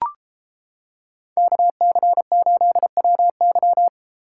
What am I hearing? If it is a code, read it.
Morse code, heard as a keyed tone, about 30 wpm, 700 Hz. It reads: KC8WY